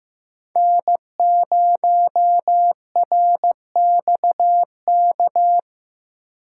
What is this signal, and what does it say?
Morse code, heard as a keyed tone, about 15 wpm, 700 Hz. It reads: N0RXK
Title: USA Call Signs - 5 Characters - 6x Character Spacing - 15wpm